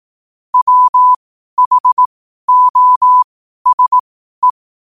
Morse code, heard as a keyed tone, about 18 wpm, spaced as for 12 wpm, 1000 Hz. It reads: WHOSE
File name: qotdfiles / 12wpm